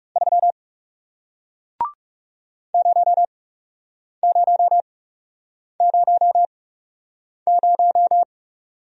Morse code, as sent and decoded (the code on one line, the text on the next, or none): ...-- / ----- ----- ----- -----
3 0000